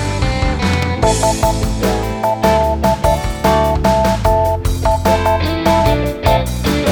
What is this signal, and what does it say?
Morse code, heard as a keyed tone, about 12 wpm, 660 Hz, with noise. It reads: S LOST E